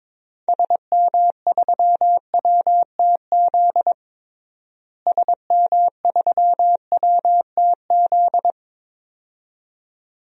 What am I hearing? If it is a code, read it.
Morse code, heard as a keyed tone, about 22 wpm, 700 Hz. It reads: SM3WTZ SM3WTZ
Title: DXCC-100 Call Signs - 6 Characters - Repeated 2x - 22wpm